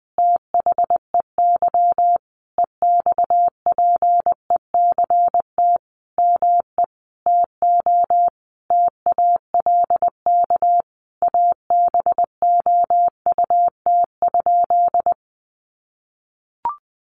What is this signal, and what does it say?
Morse code, heard as a keyed tone, about 20 wpm, 700 Hz. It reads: THEY EXPECT ME TO TALK ABOUT?
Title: Sentences from Top 1100 Words 20wpm